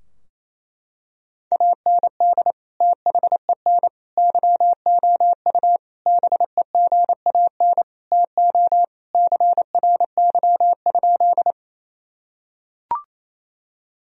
Morse code, heard as a keyed tone, about 27 wpm, 700 Hz. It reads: AND THEN YOU BEGAN TO CRY?
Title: Sentences from Top 1000 Words - Review Part 2 - 28wpm